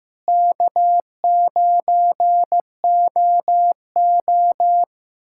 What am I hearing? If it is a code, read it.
Morse code, heard as a keyed tone, about 15 wpm, 700 Hz. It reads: K9OO
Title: K1USN SST QSO Practice - 15wpm